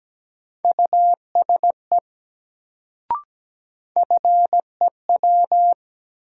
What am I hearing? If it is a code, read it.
Morse code, heard as a keyed tone, about 17 wpm, 700 Hz. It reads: USE FEW
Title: Sets of 2 Words - New Word Drill - 17wpm